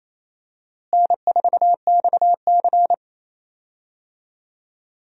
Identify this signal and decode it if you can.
Morse code, heard as a keyed tone, about 28 wpm, 700 Hz. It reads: N4XC
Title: USA Call Signs - 4 Characters - Repeated 3x - 28wpm